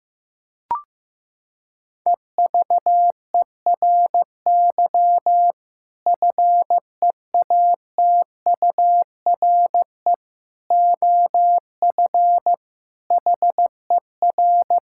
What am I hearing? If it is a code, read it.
Morse code, heard as a keyed tone, about 15 wpm, 700 Hz. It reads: EVERY FEATURE OF HER
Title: Sentences from Top 1000 Words - Review Part 5 - 15wpm